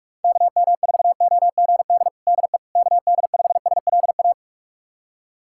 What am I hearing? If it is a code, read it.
Morse code, heard as a keyed tone, about 45 wpm, 700 Hz. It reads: KM4OGD DE KD5SLA
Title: USA Callsign DE Callsign - 6 Characters - 45wpm